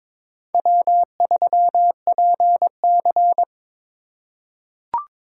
Morse code, heard as a keyed tone, about 22 wpm, 700 Hz. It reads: W3PC
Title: USA Call Signs - 4 Characters - 2x Character Spacing - 22wpm